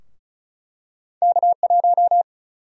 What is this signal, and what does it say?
Morse code, heard as a keyed tone, about 36 wpm, 700 Hz. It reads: K1
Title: USA Call Sign Prefixes - Repeated 2x - 35wpm